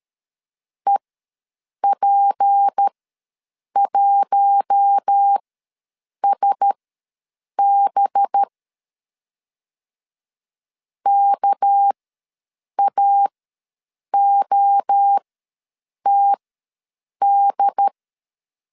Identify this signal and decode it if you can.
Morse code, heard as a keyed tone, about 13 wpm, spaced as for 7 wpm, 800 Hz. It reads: EP1SB KAOTD
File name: LEZIONI / 6 PKX0 / 6 PKX ws 12wpm 3106